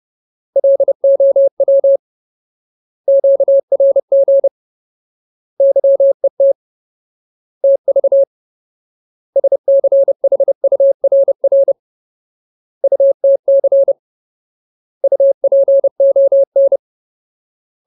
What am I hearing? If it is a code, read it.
Morse code, heard as a keyed tone, about 30 wpm, 550 Hz. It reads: LOW QRG YET TV SCHURR UTC UPON
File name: Ultimate Word List / 30WPM Ultimate